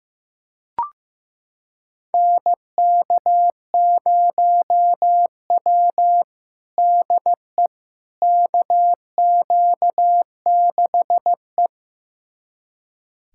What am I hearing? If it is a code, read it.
Morse code, heard as a keyed tone, about 15 wpm, 700 Hz. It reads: NK0W DE KQ6E
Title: USA Callsign DE Callsign - 4 Characters - 15wpm